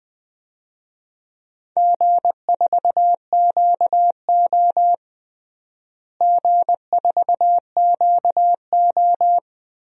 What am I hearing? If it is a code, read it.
Morse code, heard as a keyed tone, about 21 wpm, 690 Hz. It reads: G4QO G4QO
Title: DXCC-100 Call Signs - 4 Characters - Repeated 3x - 20wpm